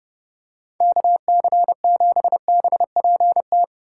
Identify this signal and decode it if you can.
Morse code, heard as a keyed tone, about 27 wpm, 700 Hz. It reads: KC7BPT